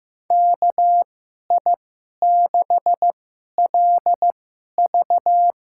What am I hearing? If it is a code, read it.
Morse code, heard as a keyed tone, about 15 wpm, spaced as for 11 wpm, 700 Hz. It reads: KI6LV
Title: USA Call Signs - 6 Characters - 2x Character Spacing - 15wpm